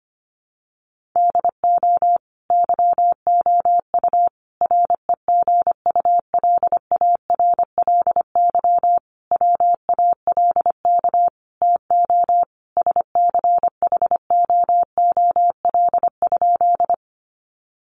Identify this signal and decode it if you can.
Morse code, heard as a keyed tone, about 25 wpm, 700 Hz. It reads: DO YOU REGULARLY WALK TO SCHOOL?
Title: Sentences from Top 1400 Words 25wpm